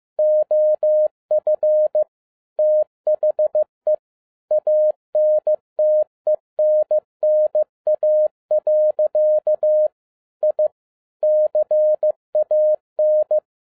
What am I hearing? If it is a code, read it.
Morse code, heard as a keyed tone, about 15 wpm, 610 Hz. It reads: OF THE ANTENNA. I CAN
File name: 2012-01-mp3 / n07-ARA JR0DIL-15wpm